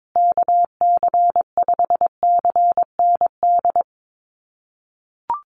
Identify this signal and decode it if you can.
Morse code, heard as a keyed tone, about 22 wpm, 700 Hz. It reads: KC5CND
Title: USA Call Signs Repeated 2x 22wpm